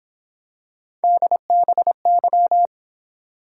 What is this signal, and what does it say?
Morse code, heard as a keyed tone, about 26 wpm, 700 Hz. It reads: DBY
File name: Speed-Racing - Three Letter-Number Combinations - 22wpm